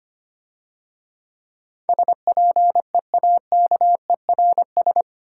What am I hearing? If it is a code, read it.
Morse code, heard as a keyed tone, about 26 wpm, 700 Hz. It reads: SPEAKERS
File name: Easy Words - Speed-Racing with Hints - 25wpm